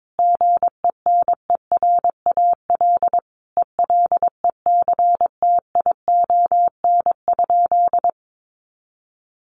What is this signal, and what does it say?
Morse code, heard as a keyed tone, about 22 wpm, 700 Hz. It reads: GENERAL ELECTION?
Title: Sentences from Top 1500 Words 22wpm